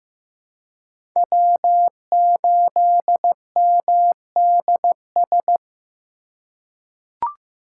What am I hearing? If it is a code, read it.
Morse code, heard as a keyed tone, about 15 wpm, 700 Hz. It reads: W8MDS
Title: USA Call Signs - 3x Character Spacing - 15wpm